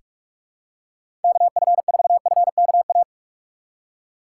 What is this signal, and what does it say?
Morse code, heard as a keyed tone, about 44 wpm, 700 Hz. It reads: KF4FKA
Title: USA Call Signs - 6 Characters - 6x Character Spacing - 45wpm